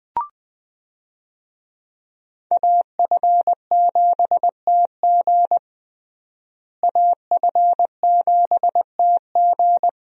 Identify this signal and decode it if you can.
Morse code, heard as a keyed tone, about 21 wpm, 700 Hz. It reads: AF7TG AF7TG
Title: USA Call Signs - 5 Characters - Repeated 3x - 20wpm